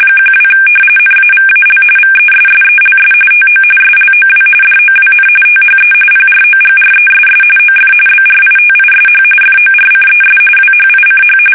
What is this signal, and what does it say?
RTTY, about 45.45 baud. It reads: 12 NAVAL VESSELS TO SLOW DOWN THE PROGRESS OF BRITISH FORCES WHICH
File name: fsk850 / usnavy1